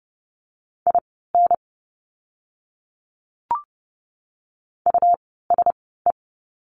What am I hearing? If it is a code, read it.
Morse code, heard as a keyed tone, about 30 wpm, spaced as for 17 wpm, 700 Hz. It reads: IN USE
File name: course / 30:10wpm / 237 Lesson 38f - Letter q - Words 30wpm